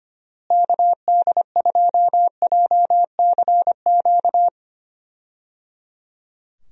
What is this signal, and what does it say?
Morse code, heard as a keyed tone, about 25 wpm, 700 Hz. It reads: KD2JCQ
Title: USA Call Signs - 6 Characters - Repeated 3x - 25wpm